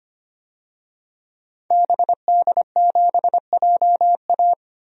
Morse code, heard as a keyed tone, about 25 wpm, 700 Hz. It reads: BD7JA
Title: DXCC-100 Call Signs - 5 Characters - 25wpm